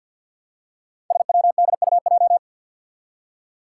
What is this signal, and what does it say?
Morse code, heard as a keyed tone, about 47 wpm, 700 Hz. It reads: SWDUJ
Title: Five Challenging Letter-Number Combinations 50wpm